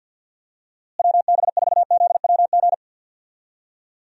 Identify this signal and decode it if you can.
Morse code, heard as a keyed tone, about 51 wpm, 700 Hz. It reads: WB4ZWG
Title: USA Call Signs - 6 Characters - Repeated 3x - 50wpm